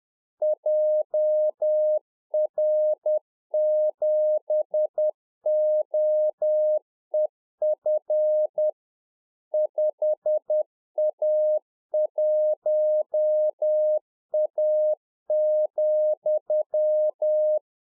Morse code, heard as a keyed tone, about 10 wpm, 610 Hz. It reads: JR7OEF 5A1A,
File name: 2011-04-mp3 / UNFOGETTABLE QSO 10WPM